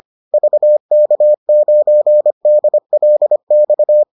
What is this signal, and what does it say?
Morse code, heard as a keyed tone, about 25 wpm, 600 Hz. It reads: VK9DLX